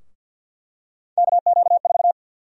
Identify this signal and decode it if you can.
Morse code, heard as a keyed tone, about 48 wpm, 700 Hz. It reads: KQ4
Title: USA Call Sign Prefixes - 50wpm